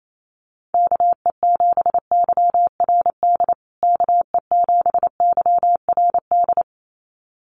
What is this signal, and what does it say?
Morse code, heard as a keyed tone, about 28 wpm, 700 Hz. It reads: KE7YRD KE7YRD
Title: USA Call Signs Repeated 2x 28wpm